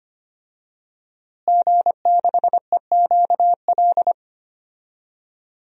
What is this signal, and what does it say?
Morse code, heard as a keyed tone, about 25 wpm, 700 Hz. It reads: G6EQL
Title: DXCC-100 Call Signs - 5 Characters - Repeated 3x - 25wpm